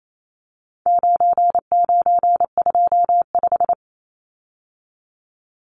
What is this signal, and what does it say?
Morse code, heard as a keyed tone, about 28 wpm, 700 Hz. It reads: 9925
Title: Numbers - 4-Digit 28wpm